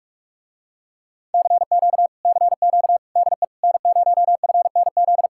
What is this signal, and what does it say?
Morse code, heard as a keyed tone, about 47 wpm, 700 Hz. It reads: CQ CQ DE N0FNZ